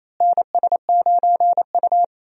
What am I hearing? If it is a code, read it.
Morse code, heard as a keyed tone, about 28 wpm, 700 Hz. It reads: NS9U
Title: USA Call Signs - 4 Characters - Repeated 2x - 28wpm